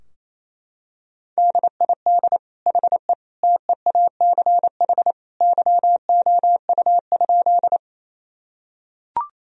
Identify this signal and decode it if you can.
Morse code, heard as a keyed tone, about 28 wpm, 700 Hz. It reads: DID HE TEACH YOU?
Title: Sentences from Top 1000 Words - Review Part 2 - 28wpm